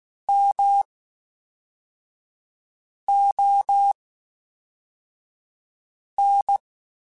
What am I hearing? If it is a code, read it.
Morse code, heard as a keyed tone, about 16 wpm, spaced as for 4 wpm, 800 Hz. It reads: MON
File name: Special INGs / Special INGs-mp3s / Fast ing @10